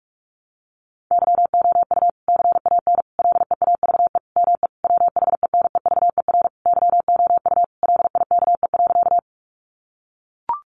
Morse code, heard as a keyed tone, about 44 wpm, 700 Hz. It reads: YOU CAN LEAVE ME WHENEVER YOU LIKE.
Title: Sentences from Top 1200 Words 45wpm